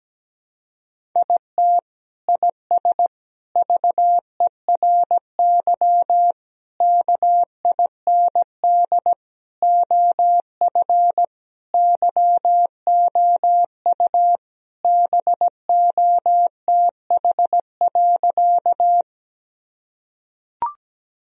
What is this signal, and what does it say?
Morse code, heard as a keyed tone, about 17 wpm, 700 Hz. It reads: IT IS VERY KIND OF YOU BOTH.
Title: Sentences from Top 1000 Words - Review Part 1 - 17wpm